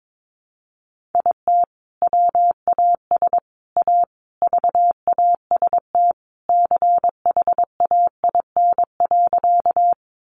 Morse code, heard as a keyed tone, about 22 wpm, 700 Hz. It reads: IT WAS A VAST CHAIN.
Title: Sentences from Top 1500 Words 22wpm